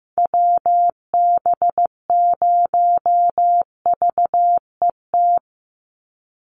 Morse code, heard as a keyed tone, about 15 wpm, 700 Hz. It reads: WB0VET
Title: USA Call Signs 15wpm